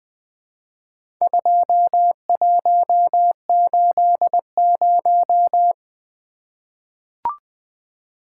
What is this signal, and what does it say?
Morse code, heard as a keyed tone, about 20 wpm, 700 Hz. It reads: 2180
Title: Four Letter-Number Combinations - 20wpm